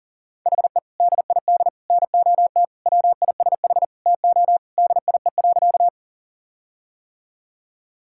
Morse code, heard as a keyed tone, about 40 wpm, 700 Hz. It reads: HE DID NOT WISH TO DIE.
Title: Sentences from Top 1600 Words 40wpm